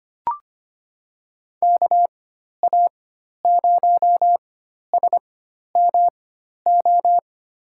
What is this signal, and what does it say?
Morse code, heard as a keyed tone, about 25 wpm, spaced as for 12 wpm, 700 Hz. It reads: KA0SMO
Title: USA Call Signs - 4x Character Spacing - 25wpm